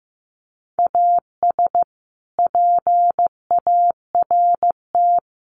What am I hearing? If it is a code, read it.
Morse code, heard as a keyed tone, about 15 wpm, 700 Hz. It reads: AS PART
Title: Sets of 3 Words 15wpm